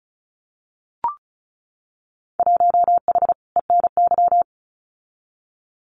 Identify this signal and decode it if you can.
Morse code, heard as a keyed tone, about 34 wpm, 700 Hz. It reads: 1H ENY
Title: Winter Field Day Exchanges 35wpm